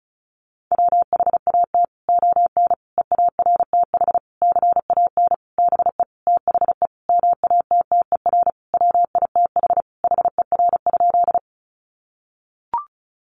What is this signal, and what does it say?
Morse code, heard as a keyed tone, about 34 wpm, 700 Hz. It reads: WHAT ON EARTH CAN BE THE MATTER WITH HER?
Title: Sentences from Top 700 Words 35wpm